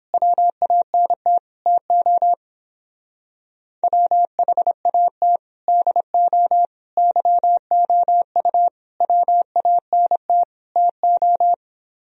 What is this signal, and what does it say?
Morse code, heard as a keyed tone, about 29 wpm, 700 Hz. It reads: WANT TO WHAT DO YOU WANT TO